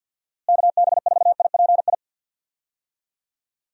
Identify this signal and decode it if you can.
Morse code, heard as a keyed tone, about 53 wpm, 700 Hz. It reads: KB4IPI
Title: USA Call Signs - 6 Characters - Repeated 2x - 50wpm